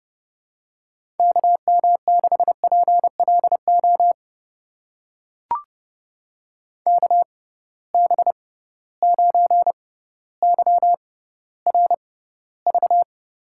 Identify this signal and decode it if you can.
Morse code, heard as a keyed tone, about 31 wpm, spaced as for 16 wpm, 700 Hz. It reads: KM6PLO KB9YRV
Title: USA Call Signs - 6 Characters - 6x Character Spacing - 30wpm